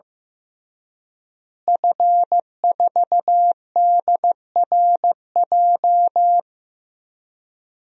Morse code, heard as a keyed tone, about 15 wpm, 700 Hz. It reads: F4DRJ
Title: DXCC-100 Call Signs - 5 Characters - Repeated 3x - 15wpm